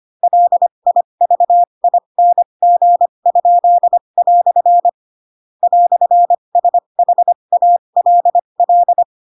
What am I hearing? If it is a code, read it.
Morse code, heard as a keyed tone, about 25 wpm, spaced as for 22 wpm, 700 Hz. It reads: LIVING?" "SHALL
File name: Pinocchio / Pinocchio ch4 18 25 ik1hge